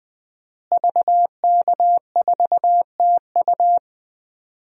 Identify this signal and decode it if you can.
Morse code, heard as a keyed tone, about 20 wpm, 700 Hz. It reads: VK4TU
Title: DXCC-100 Call Signs - 5 Characters - Repeated 3x - 20wpm